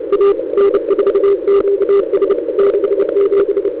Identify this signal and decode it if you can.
Morse code, heard as a keyed tone, about 29 wpm, 400 Hz, with radio noise. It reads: A N4QS 67
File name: Demo / ReceivedAudio